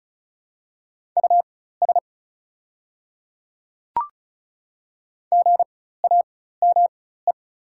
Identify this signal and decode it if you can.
Morse code, heard as a keyed tone, about 34 wpm, spaced as for 16 wpm, 700 Hz. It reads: US GAME
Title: 4x Character Spacing - Top 500 Words - 35wpm